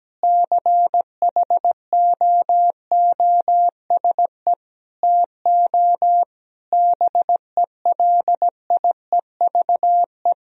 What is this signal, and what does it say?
Morse code, heard as a keyed tone, about 17 wpm, 700 Hz. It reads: CHOOSE TO BELIEVE
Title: Sentences from Top 1000 Words - Review Part 4 - 17wpm